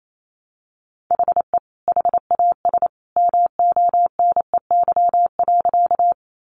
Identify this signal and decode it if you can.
Morse code, heard as a keyed tone, about 28 wpm, 700 Hz. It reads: HE HAS MONEY.
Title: Sentences from Top 400 Words 28wpm